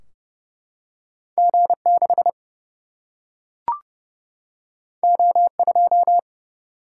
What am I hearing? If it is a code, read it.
Morse code, heard as a keyed tone, about 30 wpm, 700 Hz. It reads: G6 O2
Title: Single-Letter Single-Number 30wpm